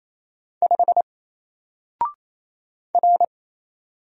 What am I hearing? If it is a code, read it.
Morse code, heard as a keyed tone, about 27 wpm, 700 Hz. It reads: 5 R
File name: Single-Letter-Number-28wpm